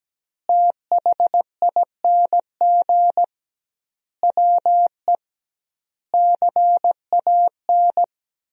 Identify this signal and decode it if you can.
Morse code, heard as a keyed tone, about 17 wpm, 700 Hz. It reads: THING WE CAN
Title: 2x Word-Spacing - Sentences from Top 200 Words - 17wpm